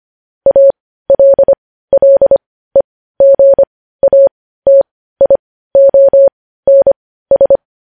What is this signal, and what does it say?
Morse code, heard as a keyed tone, about 25 wpm, spaced as for 15 wpm, 550 Hz. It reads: ALLEGATIONS
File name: podcast / News headlines-1x-25x15